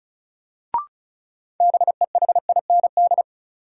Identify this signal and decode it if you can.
Morse code, heard as a keyed tone, about 36 wpm, 700 Hz. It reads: BEHIND
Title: Sentences from Top 500 Words - New Word Drill - 35wpm